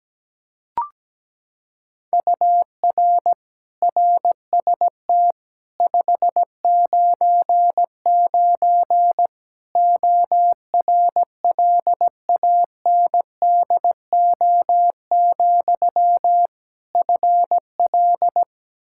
Morse code, heard as a keyed tone, about 17 wpm, 700 Hz. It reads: UR RST 599 ORLANDO, FL
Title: Practice QSO - Exchange 1 - Beginning - 17wpm